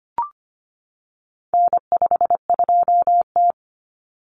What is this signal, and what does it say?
Morse code, heard as a keyed tone, about 25 wpm, 700 Hz. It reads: N52T
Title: Common Cut Numbers - 4-Digit 25wpm